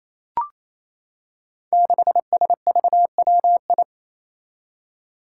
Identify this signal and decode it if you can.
Morse code, heard as a keyed tone, about 28 wpm, 700 Hz. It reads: 6SVWI
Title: Five Challenging Letter-Number Combinations 28wpm